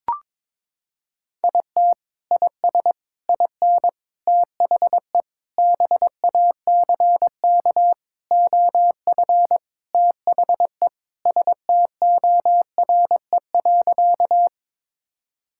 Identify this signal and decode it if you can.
Morse code, heard as a keyed tone, about 22 wpm, 700 Hz. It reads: IT IS IN THE BACK OF THE STORE.
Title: Sentences from Top 1000 Words - Review Part 5 - 22wpm